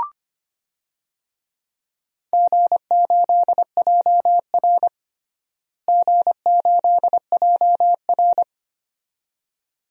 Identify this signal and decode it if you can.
Morse code, heard as a keyed tone, about 25 wpm, 700 Hz. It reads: G8JR G8JR
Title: DXCC-100 Call Signs - 4 Characters - Repeated 2x - 25wpm